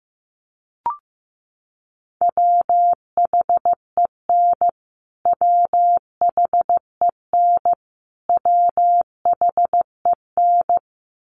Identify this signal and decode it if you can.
Morse code, heard as a keyed tone, about 15 wpm, 700 Hz. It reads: WHEN WHEN WHEN
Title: Top 300 Words Repeated 15wpm